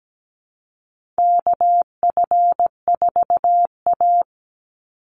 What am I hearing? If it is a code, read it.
Morse code, heard as a keyed tone, about 17 wpm, 700 Hz. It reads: KF4A
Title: Call Signs Repeated 2x - 17wpm